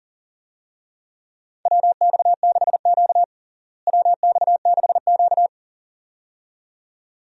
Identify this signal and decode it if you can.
Morse code, heard as a keyed tone, about 38 wpm, 700 Hz. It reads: WX6Q WX6Q
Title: USA Call Signs - 4 Characters - Repeated 2x - 40wpm